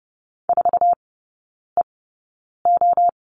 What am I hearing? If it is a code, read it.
Morse code, heard as a keyed tone, about 30 wpm, spaced as for 9 wpm, 700 Hz. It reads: 4EO
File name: Lesson 18e - Letter m - Three Characters 30wpm